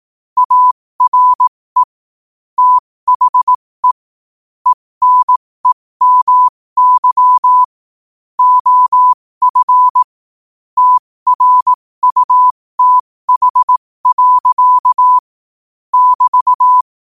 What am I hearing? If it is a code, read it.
Morse code, heard as a keyed tone, about 18 wpm, spaced as for 15 wpm, 1000 Hz. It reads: ARE THE ENEMY OF TRUTH. <BT>
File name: qotdfiles / 15wpm